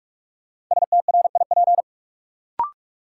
Morse code, heard as a keyed tone, about 46 wpm, 700 Hz. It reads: STRIP
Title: Sentences from Top 1600 Words - New Word Drill Alt - 45wpm